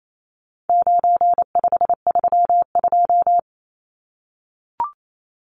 Morse code, heard as a keyed tone, about 28 wpm, 700 Hz. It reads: 9532
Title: Numbers - 4-Digit 28wpm